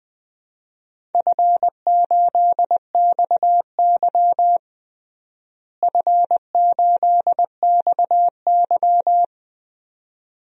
Morse code, heard as a keyed tone, about 20 wpm, 700 Hz. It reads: F8XY F8XY
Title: DXCC-100 Call Signs - 4 Characters - Repeated 2x - 20wpm